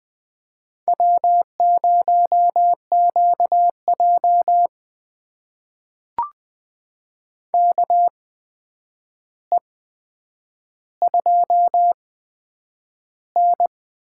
Morse code, heard as a keyed tone, about 20 wpm, spaced as for 9 wpm, 700 Hz. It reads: W0QJ KE2N
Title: USA Call Signs - 4 Characters - 8x Character Spacing - 20wpm